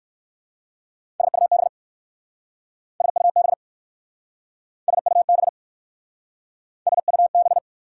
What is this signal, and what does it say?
Morse code, heard as a keyed tone, about 60 wpm, 690 Hz. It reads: SUB SUB SUB SUB